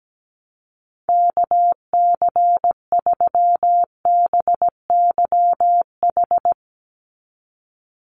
Morse code, heard as a keyed tone, about 17 wpm, 700 Hz. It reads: KC3BYH
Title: USA Call Signs 17wpm